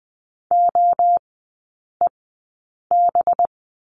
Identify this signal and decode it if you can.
Morse code, heard as a keyed tone, about 20 wpm, spaced as for 8 wpm, 700 Hz. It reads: OEB
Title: 184 Lesson 30e - Letter b - Three Characters 20wpm